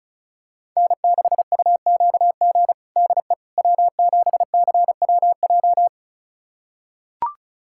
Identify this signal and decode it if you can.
Morse code, heard as a keyed tone, about 36 wpm, 700 Hz. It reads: N6UQG DE W7CWJ
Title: USA Callsign DE Callsign - 5 Characters - 35wpm